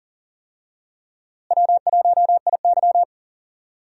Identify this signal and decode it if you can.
Morse code, heard as a keyed tone, about 40 wpm, 700 Hz. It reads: W1IY
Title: USA Call Signs - 4 Characters - Repeated 2x - 40wpm